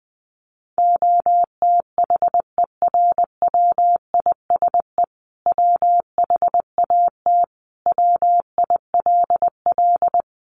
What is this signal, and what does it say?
Morse code, heard as a keyed tone, about 20 wpm, 700 Hz. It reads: OTHERWISE WHAT WILL
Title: Sentences from Top 1200 Words 20wpm